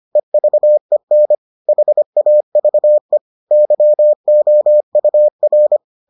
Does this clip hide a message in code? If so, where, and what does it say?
Yes: Morse code from 0.2 to 5.8 s, heard as a keyed tone, about 25 wpm, 600 Hz. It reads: EVEN HAVE YOUR